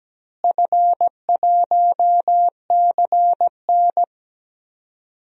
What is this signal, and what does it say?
Morse code, heard as a keyed tone, about 17 wpm, 700 Hz. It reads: F1CN
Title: DXCC-100 Call Signs - 4 Characters - Repeated 3x - 17wpm